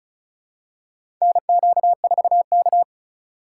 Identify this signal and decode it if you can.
Morse code, heard as a keyed tone, about 36 wpm, 700 Hz. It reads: NQ4K
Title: USA Call Signs - 4 Characters - 3x Character Spacing - 35wpm